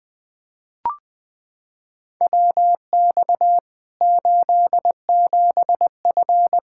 Morse code, heard as a keyed tone, about 20 wpm, 700 Hz. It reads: WX 87F
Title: K1USN SST QSO Practice 20wpm